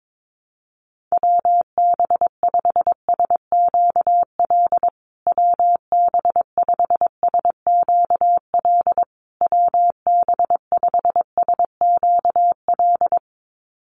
Morse code, heard as a keyed tone, about 22 wpm, 700 Hz. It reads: WB5SQL WB5SQL WB5SQL
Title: USA Call Signs Repeated 3x 22wpm